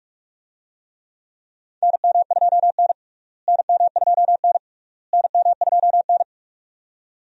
Morse code, heard as a keyed tone, about 46 wpm, 700 Hz. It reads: NM2N NM2N NM2N
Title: USA Call Signs - 4 Characters - Repeated 3x - 45wpm